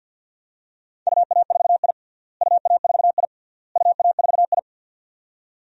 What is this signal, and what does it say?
Morse code, heard as a keyed tone, about 53 wpm, 700 Hz. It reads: UA4I UA4I UA4I
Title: DXCC-100 Call Signs - 4 Characters - Repeated 3x - 50wpm